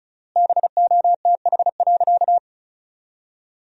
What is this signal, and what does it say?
Morse code, heard as a keyed tone, about 36 wpm, 700 Hz. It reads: BOTH.
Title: 3x Word-Spacing - Sentences from Top 300 Words - 35wpm